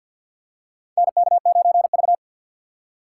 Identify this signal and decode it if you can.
Morse code, heard as a keyed tone, about 47 wpm, 700 Hz. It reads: NK9V
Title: USA Call Signs - 4 Characters - 2x Character Spacing - 50wpm